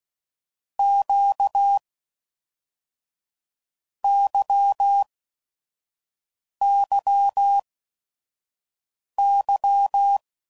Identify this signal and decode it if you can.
Morse code, heard as a keyed tone, about 16 wpm, 800 Hz. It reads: Q Y Y Y
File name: Basic Letter Groups By7Groups / LFCYQ / LFCYQ@5 f15